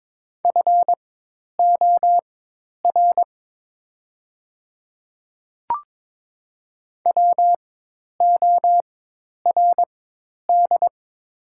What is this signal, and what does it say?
Morse code, heard as a keyed tone, about 22 wpm, spaced as for 10 wpm, 700 Hz. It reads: FOR WORD
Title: 4x Character Spacing - Top 100 Words - 22wpm